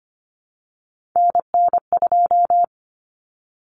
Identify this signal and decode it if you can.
Morse code, heard as a keyed tone, about 25 wpm, 700 Hz. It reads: NN2
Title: Common Cut Numbers - 3-Digit 25wpm